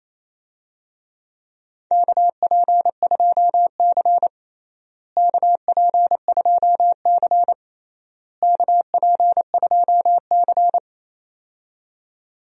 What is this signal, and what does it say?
Morse code, heard as a keyed tone, about 28 wpm, 700 Hz. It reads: KP2C KP2C KP2C
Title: DXCC-100 Call Signs - 4 Characters - Repeated 3x - 28wpm